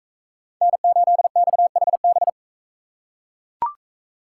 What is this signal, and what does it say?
Morse code, heard as a keyed tone, about 40 wpm, 700 Hz. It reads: N8XHB